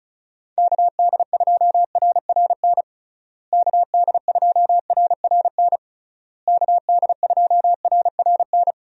Morse code, heard as a keyed tone, about 36 wpm, 700 Hz. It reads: KD2RRN KD2RRN KD2RRN
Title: USA Call Signs - 6 Characters - Repeated 3x - 35wpm